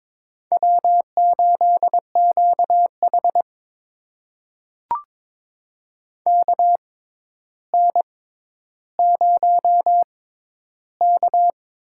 Morse code, heard as a keyed tone, about 22 wpm, spaced as for 11 wpm, 700 Hz. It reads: W8QH KN0K
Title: USA Call Signs - 4 Characters - 6x Character Spacing - 22wpm